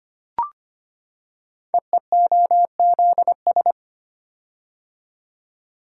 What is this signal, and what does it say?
Morse code, heard as a keyed tone, about 25 wpm, 700 Hz. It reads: EEOZS
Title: Code Groups - 5 Letters - 25wpm